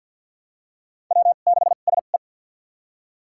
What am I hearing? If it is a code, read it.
Morse code, heard as a keyed tone, about 47 wpm, spaced as for 35 wpm, 700 Hz. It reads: W6SE
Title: USA Call Signs - 2x Character Spacing - 50wpm